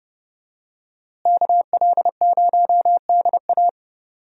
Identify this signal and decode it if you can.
Morse code, heard as a keyed tone, about 30 wpm, 700 Hz. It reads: KL0DA
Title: USA Call Signs - 5 Characters - Repeated 2x - 30wpm